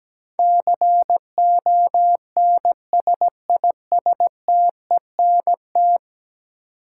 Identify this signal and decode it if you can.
Morse code, heard as a keyed tone, about 17 wpm, 700 Hz. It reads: CONSISTENT
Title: Sentences from Top 1500 Words - New Word Drill - 17wpm